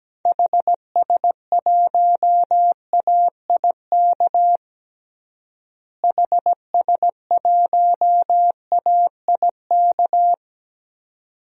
Morse code, heard as a keyed tone, about 17 wpm, 700 Hz. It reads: HS1AIK HS1AIK
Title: DXCC-100 Call Signs - 6 Characters - Repeated 2x - 17wpm